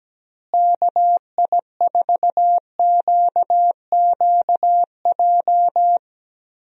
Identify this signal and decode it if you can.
Morse code, heard as a keyed tone, about 17 wpm, 700 Hz. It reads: KI4QQJ
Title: USA Call Signs - 6 Characters - Repeated 3x - 17wpm